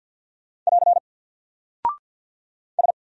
Morse code, heard as a keyed tone, about 50 wpm, 700 Hz. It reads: <AR> S
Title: Single Character (Letter:Number:Punctuation:Prosign) Alt 50wpm